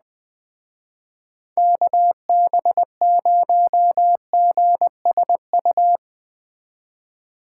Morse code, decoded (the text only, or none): KB0GSU